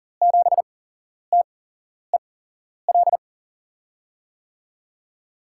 Morse code, heard as a keyed tone, about 40 wpm, spaced as for 11 wpm, 700 Hz. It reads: 7TEL
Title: USA Call Signs - 8x Character Spacing - 40wpm